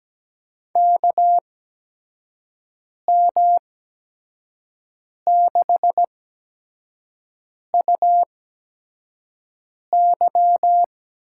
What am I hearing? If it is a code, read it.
Morse code, heard as a keyed tone, about 17 wpm, spaced as for 5 wpm, 700 Hz. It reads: KM6UY
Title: USA Call Signs - 6 Characters - 8x Character Spacing - 17wpm